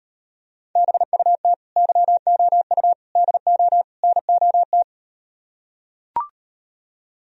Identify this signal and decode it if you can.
Morse code, heard as a keyed tone, about 38 wpm, 700 Hz. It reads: BUT YOU DO NOT